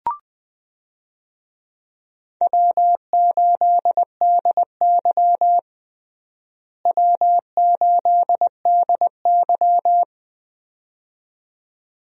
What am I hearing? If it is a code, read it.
Morse code, heard as a keyed tone, about 20 wpm, 700 Hz. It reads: W8DY W8DY
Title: USA Call Signs - 4 Characters - Repeated 2x - 20wpm